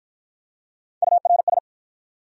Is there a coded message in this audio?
Yes: Morse code from 1.0 to 1.6 s, heard as a keyed tone, about 45 wpm, 700 Hz. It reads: URS